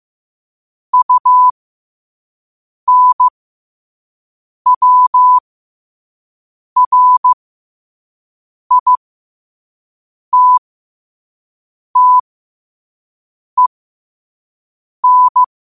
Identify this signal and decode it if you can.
Morse code, heard as a keyed tone, about 15 wpm, spaced as for 5 wpm, 1000 Hz. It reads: UNWRITTEN
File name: qotdfiles / 5wpm